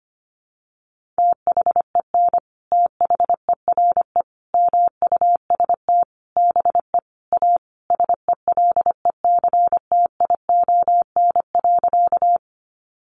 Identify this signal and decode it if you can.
Morse code, heard as a keyed tone, about 25 wpm, 700 Hz. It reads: THEN THERE MUST BE A SELECTION.
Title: Sentences from Top 1400 Words 25wpm